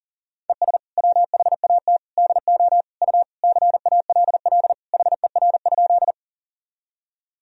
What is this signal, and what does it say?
Morse code, heard as a keyed tone, about 40 wpm, 700 Hz. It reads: ES WHAT DO U CALL HER?
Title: Sentences from Top 100 Words and Abbreviations 40wpm